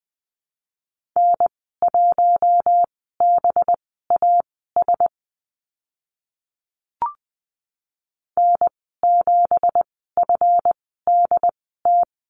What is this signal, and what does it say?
Morse code, heard as a keyed tone, about 20 wpm, spaced as for 14 wpm, 700 Hz. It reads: N1BAS N7FDT
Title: 219 Lesson 35g - Number 0 - Call Signs 20wpm